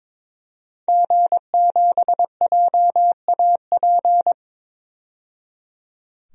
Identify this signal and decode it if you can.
Morse code, heard as a keyed tone, about 22 wpm, 700 Hz. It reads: G7JAP
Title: DXCC-100 Call Signs - 5 Characters - 22wpm